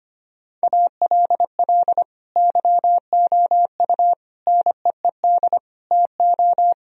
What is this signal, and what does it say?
Morse code, heard as a keyed tone, about 25 wpm, 700 Hz. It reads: ALL YOU NEED TO
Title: Sentences from Top 1000 Words - Review Part 4 - 25wpm